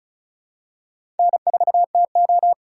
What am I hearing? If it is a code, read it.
Morse code, heard as a keyed tone, about 34 wpm, 700 Hz. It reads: N4TO
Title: USA Call Signs - 4 Characters - 3x Character Spacing - 35wpm